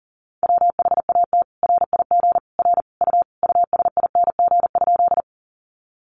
Morse code, heard as a keyed tone, about 40 wpm, 700 Hz. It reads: WHAT RIG R U USING?
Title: Common QSO Phrases 40wpm